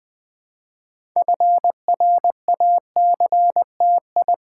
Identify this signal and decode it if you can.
Morse code, heard as a keyed tone, about 20 wpm, 700 Hz. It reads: FRACTI
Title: Build A Word - 3 Letters At A Time - 8-Letter Words - 20wpm